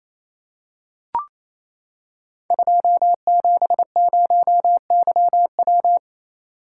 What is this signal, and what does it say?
Morse code, heard as a keyed tone, about 28 wpm, 700 Hz. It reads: 270YW
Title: Code Groups - 5 Letter:Numbers - 28wpm